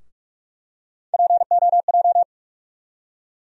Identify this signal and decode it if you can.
Morse code, heard as a keyed tone, about 43 wpm, 700 Hz. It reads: POJ
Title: USA Call Sign Postfixes - 3 Letters 45wpm